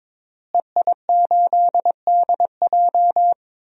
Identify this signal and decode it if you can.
Morse code, heard as a keyed tone, about 22 wpm, 700 Hz. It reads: EI8DJ
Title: DXCC-100 Call Signs - 5 Characters - Repeated 2x - 22wpm